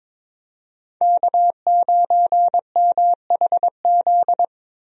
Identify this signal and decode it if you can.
Morse code, heard as a keyed tone, about 22 wpm, 700 Hz. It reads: K9MHZ
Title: USA Call Signs - 5 Characters - Repeated 2x - 22wpm